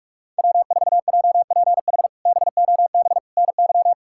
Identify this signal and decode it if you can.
Morse code, heard as a keyed tone, about 41 wpm, 700 Hz. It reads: W4JPH BOB NY
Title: K1USN SST QSO Practice - 45wpm